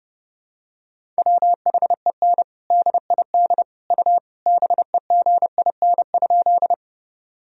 Morse code, heard as a keyed tone, about 30 wpm, 700 Hz. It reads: WHEN DID U BEGIN?
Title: Sentences from Top 300 Words and Abbreviations 30wpm